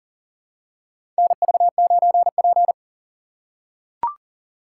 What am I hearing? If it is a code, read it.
Morse code, heard as a keyed tone, about 40 wpm, 700 Hz. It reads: NV9P
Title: USA Call Signs - 4x Character Spacing - 40wpm